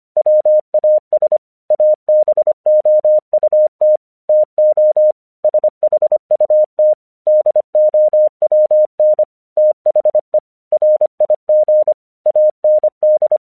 Morse code, heard as a keyed tone, about 25 wpm, 610 Hz. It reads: WAS ABOUT TO SHUT DOWN THE RIG AND